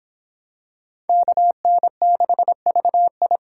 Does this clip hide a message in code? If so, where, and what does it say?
Yes: Morse code from 1.1 to 3.4 s, heard as a keyed tone, about 25 wpm, 700 Hz. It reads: KN6VI